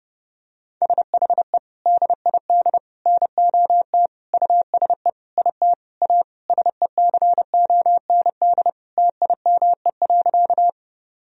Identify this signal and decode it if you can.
Morse code, heard as a keyed tone, about 30 wpm, 700 Hz. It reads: SHE DID NOT USE IT A SECOND TIME.
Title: Sentences from Top 1000 Words - Review Part 1 - 30wpm